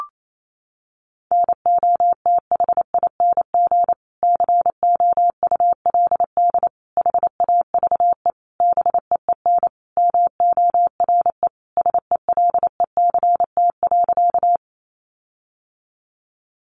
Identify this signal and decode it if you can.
Morse code, heard as a keyed tone, about 28 wpm, 700 Hz. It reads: NOTHING COULD HAVE BEEN MORE SELECT.
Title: Sentences from Top 1300 Words 28wpm